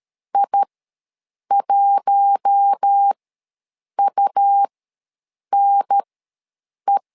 Morse code, heard as a keyed tone, about 13 wpm, spaced as for 7 wpm, 800 Hz. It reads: I1UNE
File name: LEZIONI / 4 Q1 ws 12wpm 2129